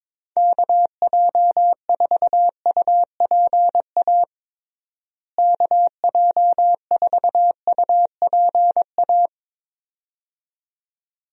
Morse code, heard as a keyed tone, about 21 wpm, 700 Hz. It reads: KJ4UPA KJ4UPA
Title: USA Call Signs - 6 Characters - Repeated 2x - 22wpm